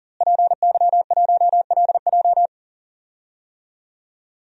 Morse code, heard as a keyed tone, about 40 wpm, 700 Hz. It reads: PY1LJ
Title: DXCC-100 Call Signs - 5 Characters - Repeated 2x - 40wpm